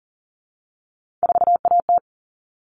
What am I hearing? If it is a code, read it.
Morse code, heard as a keyed tone, about 40 wpm, 700 Hz. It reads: 4AT
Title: All Cut Numbers - 3-Digit 40wpm